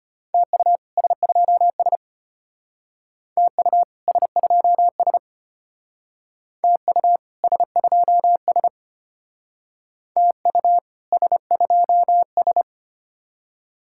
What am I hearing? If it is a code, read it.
Morse code, heard as a keyed tone, about 33 wpm, 700 Hz. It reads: TU S2S TU S2S TU S2S TU S2S